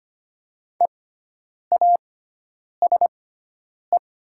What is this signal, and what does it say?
Morse code, heard as a keyed tone, about 25 wpm, spaced as for 9 wpm, 700 Hz. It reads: EASE
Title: 6x Character Spacing - Top 300 Words - 25wpm